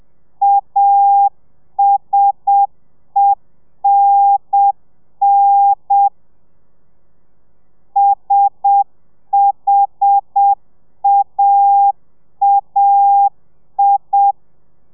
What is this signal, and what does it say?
Morse code, heard as a keyed tone, about 7 wpm, 800 Hz. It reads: ASENN SHAAI